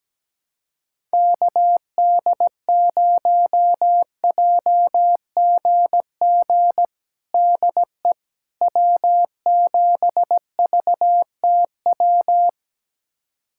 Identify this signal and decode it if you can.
Morse code, heard as a keyed tone, about 17 wpm, 700 Hz. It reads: KD0JGG DE W7VTW
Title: USA Callsign DE Callsign 17wpm